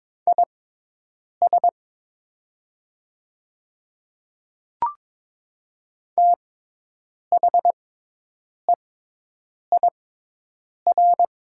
Morse code, heard as a keyed tone, about 22 wpm, spaced as for 8 wpm, 700 Hz. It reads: IS THEIR
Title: 6x Character Spacing - Top 300 Words - 22wpm